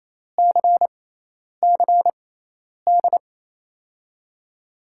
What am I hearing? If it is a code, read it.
Morse code, heard as a keyed tone, about 28 wpm, spaced as for 10 wpm, 700 Hz. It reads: CCD
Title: USA Call Signs - 5 Characters - 6x Character Spacing - 28wpm